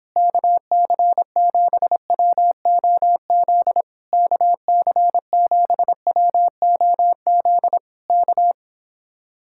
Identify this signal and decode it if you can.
Morse code, heard as a keyed tone, about 26 wpm, 700 Hz. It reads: KC7WOZ KC7WOZ K